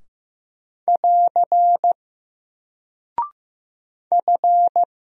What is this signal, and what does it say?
Morse code, heard as a keyed tone, about 15 wpm, 700 Hz. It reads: <AR> F
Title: Single Character (Letter:Number:Punctuation:Prosign) 15wpm